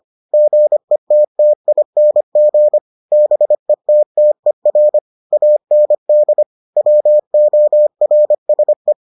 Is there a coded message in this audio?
Yes: Morse code from 0.3 to 8.9 s, heard as a keyed tone, about 25 wpm, 600 Hz. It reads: GETTING BETTER AND WORSE